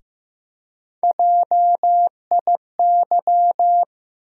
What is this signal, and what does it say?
Morse code, heard as a keyed tone, about 15 wpm, 700 Hz. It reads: JIY
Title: USA Call Sign Postfixes - 3 Letters 15wpm